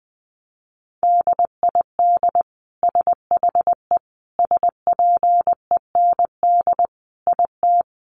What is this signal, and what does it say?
Morse code, heard as a keyed tone, about 20 wpm, 700 Hz. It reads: DID SHE SPEND IT
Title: Sentences from Top 800 Words 20wpm